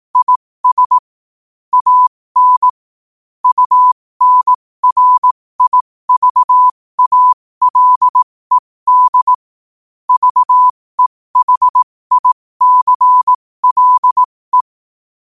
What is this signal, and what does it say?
Morse code, heard as a keyed tone, about 18 wpm, spaced as for 15 wpm, 1000 Hz. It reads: IS AN UNRIVALED VEHICLE
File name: qotdfiles / 15wpm